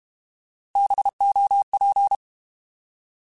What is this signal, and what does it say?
Morse code, heard as a keyed tone, about 32 wpm, 800 Hz. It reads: BOP
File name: Alphabetical / Alphabetical mp3s / Bs @30